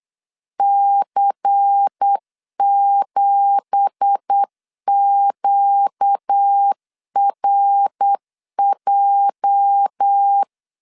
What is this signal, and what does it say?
Morse code, heard as a keyed tone, about 8 wpm, 800 Hz. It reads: C7QRJ
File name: 8wpm